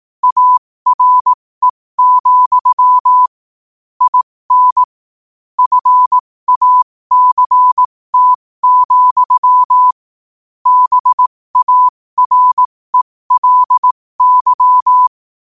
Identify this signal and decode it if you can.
Morse code, heard as a keyed tone, about 18 wpm, spaced as for 15 wpm, 1000 Hz. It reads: ARE, IN FACT, BARELY
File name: qotdfiles / 15wpm